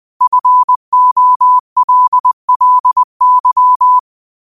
Morse code, heard as a keyed tone, about 20 wpm, 1000 Hz. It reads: FOLLY
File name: qotdfiles / 20wpm